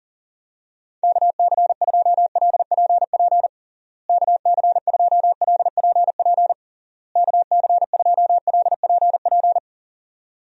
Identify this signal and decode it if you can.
Morse code, heard as a keyed tone, about 40 wpm, 700 Hz. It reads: KC2LPP KC2LPP KC2LPP
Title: USA Call Signs - 6 Characters - Repeated 3x - 40wpm